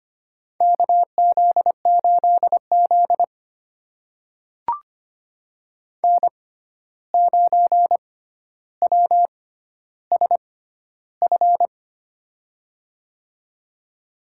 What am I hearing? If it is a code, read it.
Morse code, heard as a keyed tone, about 25 wpm, spaced as for 12 wpm, 700 Hz. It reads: KZ8Z N9WSF
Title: USA Call Signs - 6x Character Spacing - 25wpm